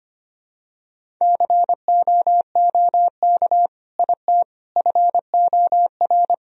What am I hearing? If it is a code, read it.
Morse code, heard as a keyed tone, about 25 wpm, 700 Hz. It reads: COOK IT FOR
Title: Sentences from Top 1000 Words - Review Part 4 - 25wpm